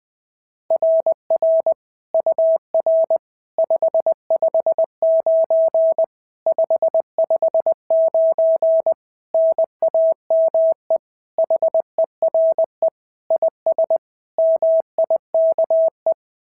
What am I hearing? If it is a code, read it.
Morse code, heard as a keyed tone, about 20 wpm, 690 Hz. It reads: RR UR 559 559 NAME HERE IS MIKE
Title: SOTA QSO Practice 20wpm